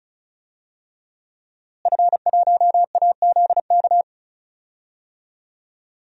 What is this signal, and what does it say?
Morse code, heard as a keyed tone, about 34 wpm, 700 Hz. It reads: F1AZK
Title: DXCC-100 Call Signs - 5 Characters - Repeated 2x - 35wpm